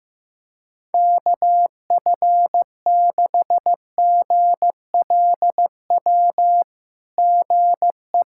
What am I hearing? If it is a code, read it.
Morse code, heard as a keyed tone, about 16 wpm, 690 Hz. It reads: KF6GLW GE
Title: SOTA QSO Practice 15wpm